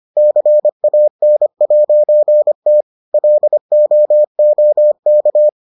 Morse code, heard as a keyed tone, about 25 wpm, 600 Hz. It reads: CAN'T LOOK